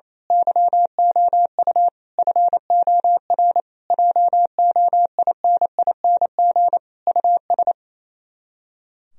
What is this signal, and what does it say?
Morse code, heard as a keyed tone, about 27 wpm, 700 Hz. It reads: YOU FOR JOINING US
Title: Speed-Racing - Sets of 5 Words 28wpm